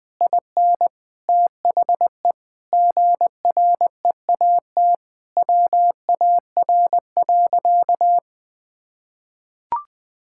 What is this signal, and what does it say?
Morse code, heard as a keyed tone, about 21 wpm, 700 Hz. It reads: IN THE GREAT WAR.
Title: Sentences from Top 1600 Words 20wpm